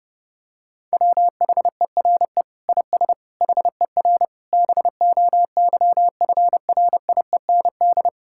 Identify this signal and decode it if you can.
Morse code, heard as a keyed tone, about 32 wpm, 700 Hz. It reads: WHERE IS HER BOYFRIEND
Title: Sentences from Top 1000 Words - Review Part 5 - 30wpm